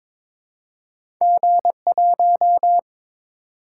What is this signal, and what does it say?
Morse code, heard as a keyed tone, about 22 wpm, 700 Hz. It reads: G1
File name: practice / 22wpm / Two Letter-Number Combinations - 22wpm